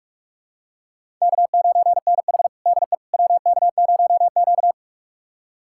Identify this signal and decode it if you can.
Morse code, heard as a keyed tone, about 41 wpm, 700 Hz. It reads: K9NH DE WK0Q
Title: USA Callsign DE Callsign - 4 Characters - 45wpm